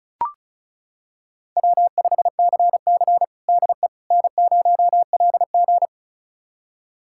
Morse code, heard as a keyed tone, about 34 wpm, 700 Hz. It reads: W5CC DE N0LG
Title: USA Callsign DE Callsign - 4 Characters - 35wpm